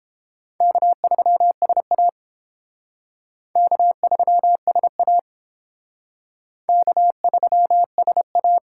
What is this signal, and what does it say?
Morse code, heard as a keyed tone, about 29 wpm, 700 Hz. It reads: K3SA K3SA K3SA